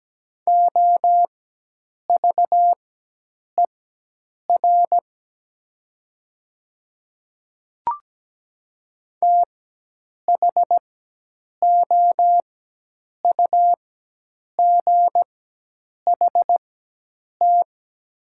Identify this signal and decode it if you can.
Morse code, heard as a keyed tone, about 17 wpm, spaced as for 8 wpm, 700 Hz. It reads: OVER THOUGHT
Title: 4x Character Spacing - Top 500 Words - 17wpm